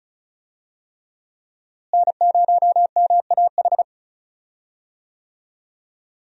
Morse code, heard as a keyed tone, about 36 wpm, 700 Hz. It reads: N0MAH
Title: USA Call Signs - 5 Characters - Repeated 3x - 35wpm